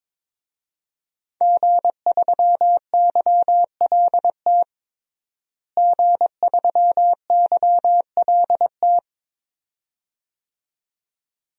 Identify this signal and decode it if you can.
Morse code, heard as a keyed tone, about 21 wpm, 700 Hz. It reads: G3YLT G3YLT
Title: DXCC-100 Call Signs - 5 Characters - Repeated 2x - 22wpm